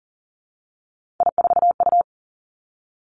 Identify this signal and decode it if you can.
Morse code, heard as a keyed tone, about 42 wpm, 690 Hz. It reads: I4U
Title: Three Challenging Letter-Number Combinations 40wpm